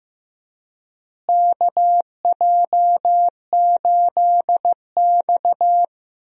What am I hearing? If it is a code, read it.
Morse code, heard as a keyed tone, about 15 wpm, 700 Hz. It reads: KJ8X
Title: USA Call Signs - 4 Characters - Repeated 3x - 15wpm